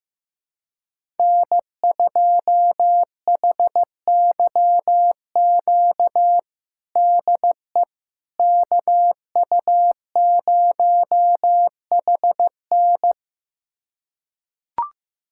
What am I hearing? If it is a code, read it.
Morse code, heard as a keyed tone, about 15 wpm, 700 Hz. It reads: N2HYQ DE KU0HN
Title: USA Callsign DE Callsign - 5 Characters - 15wpm